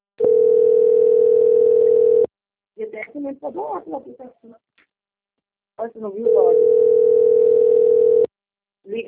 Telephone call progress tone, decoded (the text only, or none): ringback tone